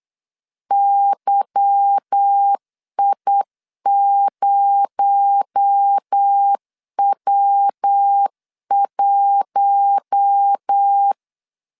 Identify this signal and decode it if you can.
Morse code, heard as a keyed tone, about 8 wpm, 800 Hz. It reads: YI0W1
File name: LEZIONI / 8 YJ45 / 8 Y 8wpm 3245